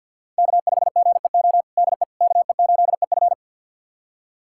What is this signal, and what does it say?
Morse code, heard as a keyed tone, about 51 wpm, 700 Hz. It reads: K5GEO DE KE8EF
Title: USA Callsign DE Callsign - 5 Characters - 50wpm